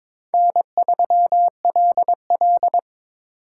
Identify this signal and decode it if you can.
Morse code, heard as a keyed tone, about 26 wpm, 700 Hz. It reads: N3LL
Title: Speed Racing - USA 1x2 Call Signs 22wpm